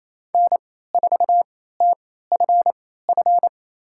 Morse code, heard as a keyed tone, about 28 wpm, spaced as for 16 wpm, 700 Hz. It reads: N4TFF
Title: USA Call Signs - 3x Character Spacing - 28wpm